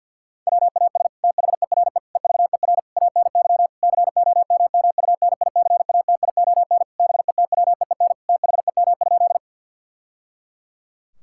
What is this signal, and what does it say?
Morse code, heard as a keyed tone, about 50 wpm, 700 Hz. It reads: WAS THERE EVER ANY COMMUNICATION BETWEEN THEM?
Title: Sentences from Top 1000 Words - Review Part 6 - 50wpm